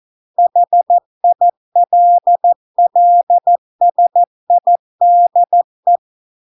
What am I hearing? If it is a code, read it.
Morse code, heard as a keyed tone, about 14 wpm, 700 Hz. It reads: HILLSIDE